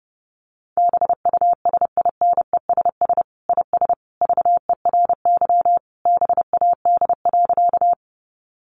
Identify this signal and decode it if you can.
Morse code, heard as a keyed tone, about 30 wpm, 700 Hz. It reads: BUSINESS IS VERY BAD.
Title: Sentences from Top 700 Words 30wpm